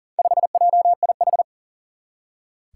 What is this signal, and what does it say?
Morse code, heard as a keyed tone, about 40 wpm, 700 Hz. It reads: 5JIH
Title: Four Challenging Letter-Number Combinations 40wpm